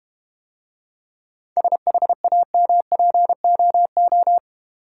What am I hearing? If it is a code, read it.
Morse code, heard as a keyed tone, about 30 wpm, 700 Hz. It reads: SHAMPOO
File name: Easy Words - Speed-Racing with Hints - 25wpm